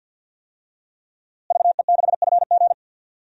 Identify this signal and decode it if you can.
Morse code, heard as a keyed tone, about 52 wpm, 700 Hz. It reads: VE6FG
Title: DXCC-100 Call Signs - 5 Characters - Repeated 3x - 50wpm